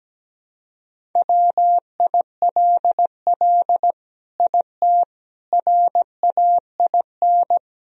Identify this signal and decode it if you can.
Morse code, heard as a keyed tone, about 17 wpm, 700 Hz. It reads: WILL IT RAIN
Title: Sentences from Top 1000 Words - Review Part 1 - 17wpm